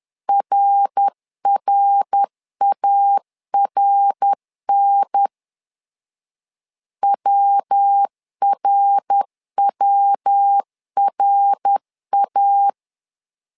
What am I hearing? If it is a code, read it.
Morse code, heard as a keyed tone, about 10 wpm, 800 Hz. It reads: RRARN WRWRA
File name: LEZIONI / 3 MWRO / 3 MWR 10wpm 1952